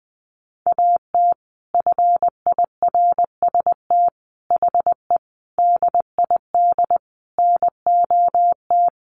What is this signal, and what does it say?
Morse code, heard as a keyed tone, about 20 wpm, 700 Hz. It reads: AT FIRST HE DID NOT
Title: Sentences from Top 600 Words 20wpm